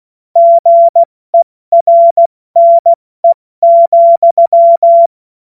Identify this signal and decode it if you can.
Morse code, heard as a keyed tone, about 16 wpm, 680 Hz. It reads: GERNE,
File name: Morsekurs / Witze / Witz 05. Oktober 2021, 70 F80